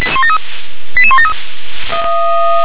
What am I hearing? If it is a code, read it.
Five-tone selective call (ZVEI), heard as quick runs of five tones, tones of about 70 ms. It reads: 88173
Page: SDR/ZVEI